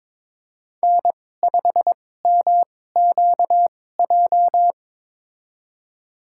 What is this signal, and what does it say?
Morse code, heard as a keyed tone, about 21 wpm, spaced as for 16 wpm, 700 Hz. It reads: N5MQJ
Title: USA Call Signs - 2x Character Spacing - 22wpm